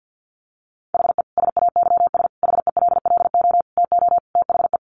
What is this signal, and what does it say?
Morse code, heard as a keyed tone, about 50 wpm, 700 Hz. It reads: HE SAYS HELLO TO THE
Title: Additional Easy Phrases 50wpm